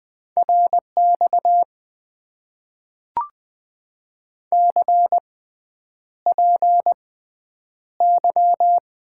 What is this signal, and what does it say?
Morse code, heard as a keyed tone, about 20 wpm, spaced as for 9 wpm, 700 Hz. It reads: RX CPY
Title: Top 90 QSO Elements - 6x Character Spacing - 20wpm